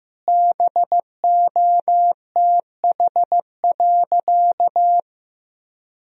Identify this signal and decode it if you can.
Morse code, heard as a keyed tone, about 15 wpm, 700 Hz. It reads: BOTH.
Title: Sentences from Top 1000 Words - Review Part 1 - 15wpm